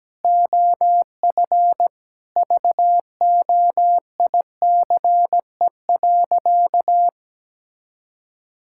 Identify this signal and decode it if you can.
Morse code, heard as a keyed tone, about 17 wpm, 700 Hz. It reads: OF VOICE.
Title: Sentences from Top 1600 Words 17wpm